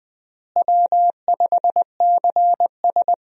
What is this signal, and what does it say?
Morse code, heard as a keyed tone, about 20 wpm, 700 Hz. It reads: W5CS
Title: USA Call Signs - 4 Characters - Repeated 2x - 20wpm